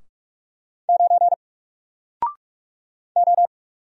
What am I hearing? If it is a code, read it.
Morse code, heard as a keyed tone, about 45 wpm, 700 Hz. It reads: 9 O
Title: Single Letter-Number Alt 45wpm